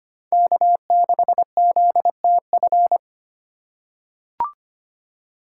Morse code, heard as a keyed tone, about 25 wpm, 700 Hz. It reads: K6ZTF
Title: USA Call Signs - 5 Characters - 3x Character Spacing - 25wpm